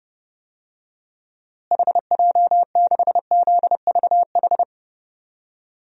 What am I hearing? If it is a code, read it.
Morse code, heard as a keyed tone, about 30 wpm, 700 Hz. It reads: HJ6ZVH
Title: DXCC-100 Call Signs - 6 Characters - Repeated 3x - 30wpm